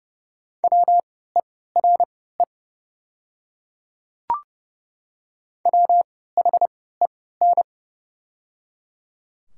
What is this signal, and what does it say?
Morse code, heard as a keyed tone, about 30 wpm, spaced as for 17 wpm, 700 Hz. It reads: WERE WHEN
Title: 3x Character Spacing - Top 100 Words - 30wpm